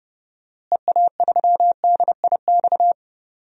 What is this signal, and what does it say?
Morse code, heard as a keyed tone, about 32 wpm, 700 Hz. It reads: EA3DIX
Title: DXCC-100 Call Signs - 6 Characters - Repeated 3x - 30wpm